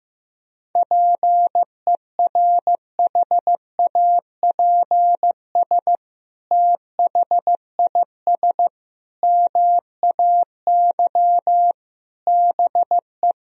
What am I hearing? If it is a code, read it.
Morse code, heard as a keyed tone, about 15 wpm, 700 Hz. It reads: PERHAPS THIS MAY BE
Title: Sentences from Top 1000 Words - Review Part 2 - 15wpm